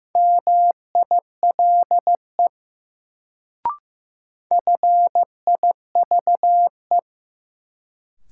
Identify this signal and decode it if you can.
Morse code, heard as a keyed tone, about 15 wpm, 700 Hz. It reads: MILE FIVE
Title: Sentences from Top 400 Words - New Word Drill - 15wpm